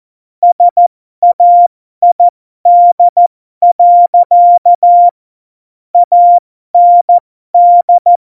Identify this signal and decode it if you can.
Morse code, heard as a keyed tone, about 14 wpm, spaced as for 12 wpm, 700 Hz. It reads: SAID. AND